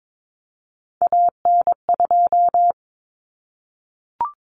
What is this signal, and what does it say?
Morse code, heard as a keyed tone, about 22 wpm, 700 Hz. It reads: AN2
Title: Common Cut Numbers - 3-Digit 22wpm